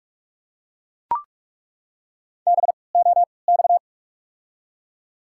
Morse code, heard as a keyed tone, about 46 wpm, spaced as for 26 wpm, 690 Hz. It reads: BOX
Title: 3x Character Spacing - Top 500 Words - 45wpm